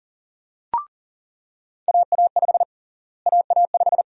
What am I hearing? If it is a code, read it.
Morse code, heard as a keyed tone, about 40 wpm, 700 Hz. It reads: AA5 AA5
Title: USA Call Sign Prefixes - Repeated 2x - 40wpm